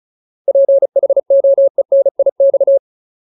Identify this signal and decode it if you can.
Morse code, heard as a keyed tone, about 34 wpm, 550 Hz. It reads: PHOENIX
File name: Ultimate Word List / 35WPM Ultimate